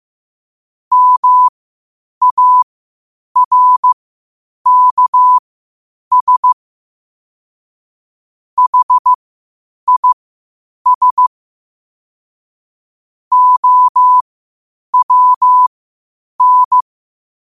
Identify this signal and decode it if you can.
Morse code, heard as a keyed tone, about 15 wpm, spaced as for 8 wpm, 1000 Hz. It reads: MARKS HIS OWN
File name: qotdfiles / 8wpm